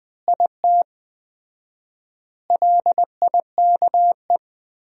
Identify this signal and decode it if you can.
Morse code, heard as a keyed tone, about 20 wpm, 700 Hz. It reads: IT LIKE
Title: 4x Word-Spacing - Sentences from Top 300 Words - 20wpm